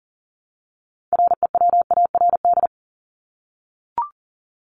Morse code, heard as a keyed tone, about 40 wpm, 700 Hz. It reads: REWARD
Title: Top 1751-2000 Words 40wpm